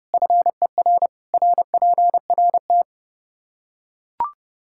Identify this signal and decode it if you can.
Morse code, heard as a keyed tone, about 29 wpm, 700 Hz. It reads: FER RPRT
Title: Practice QSO - Exchange 2 - Info - 30wpm